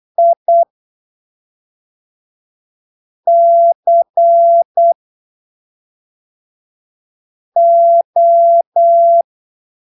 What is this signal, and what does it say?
Morse code, heard as a keyed tone, about 8 wpm, 680 Hz. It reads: I C O
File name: arquivos / texto / texto 12 08ppm